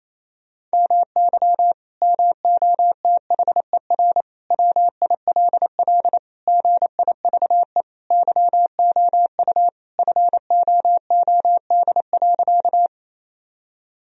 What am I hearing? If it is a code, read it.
Morse code, heard as a keyed tone, about 28 wpm, 700 Hz. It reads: MY MOTHER WILL GIVE YOU FOOD.
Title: Sentences from Top 1000 Words - Review Part 1 - 28wpm